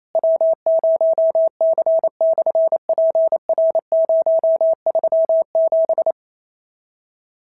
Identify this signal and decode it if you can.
Morse code, heard as a keyed tone, about 28 wpm, 650 Hz. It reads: W0C/PR037
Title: SOTA QSO Practice 28wpm